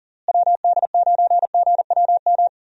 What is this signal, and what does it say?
Morse code, heard as a keyed tone, about 40 wpm, 700 Hz. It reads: WD9GWM
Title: USA Call Signs - 6 Characters - Repeated 2x - 40wpm